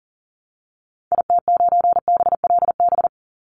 Morse code, heard as a keyed tone, about 40 wpm, 700 Hz. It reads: IT9BLB
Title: Call Signs Repeated 3x - 40wpm